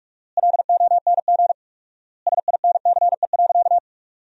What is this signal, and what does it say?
Morse code, heard as a keyed tone, about 41 wpm, 700 Hz. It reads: LONG SINCE.
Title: 4x Word-Spacing - Sentences from Top 300 Words - 45wpm